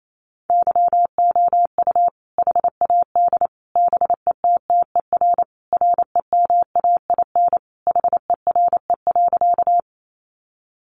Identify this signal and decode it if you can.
Morse code, heard as a keyed tone, about 28 wpm, 700 Hz. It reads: YOU HAD BETTER REMAIN HERE.
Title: Sentences from Top 1200 Words 28wpm